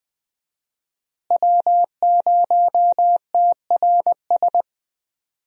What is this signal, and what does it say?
Morse code, heard as a keyed tone, about 21 wpm, 700 Hz. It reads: W0TRS
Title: USA Call Signs - 5 Characters - Repeated 2x - 20wpm